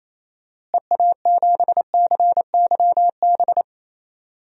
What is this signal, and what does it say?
Morse code, heard as a keyed tone, about 28 wpm, 700 Hz. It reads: EA7CYB
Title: DXCC-100 Call Signs - 6 Characters - Repeated 3x - 28wpm